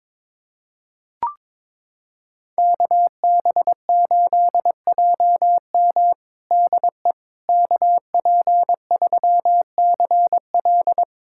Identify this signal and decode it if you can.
Morse code, heard as a keyed tone, about 22 wpm, 700 Hz. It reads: KB8JM DE KP3CL
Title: USA Callsign DE Callsign - 5 Characters - 22wpm